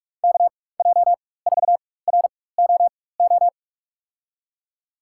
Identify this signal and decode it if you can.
Morse code, heard as a keyed tone, about 42 wpm, spaced as for 20 wpm, 700 Hz. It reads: KJ4ROO
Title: USA Call Signs - 6 Characters - 4x Character Spacing - 45wpm